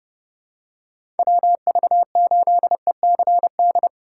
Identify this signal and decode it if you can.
Morse code, heard as a keyed tone, about 30 wpm, 700 Hz. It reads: WV8ECD
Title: USA Call Signs - 6 Characters - 4x Character Spacing - 30wpm